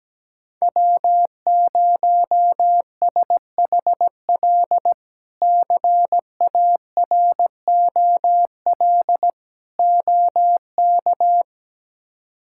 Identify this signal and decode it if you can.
Morse code, heard as a keyed tone, about 17 wpm, 700 Hz. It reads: W0SHL CAROL OK
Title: K1USN SST QSO Practice - 17wpm